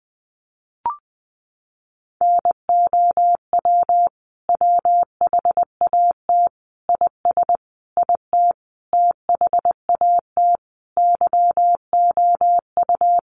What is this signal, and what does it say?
Morse code, heard as a keyed tone, about 20 wpm, 700 Hz. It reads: NOW WHAT IS IT THAT YOU
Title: Sentences from Top 200 Words 20wpm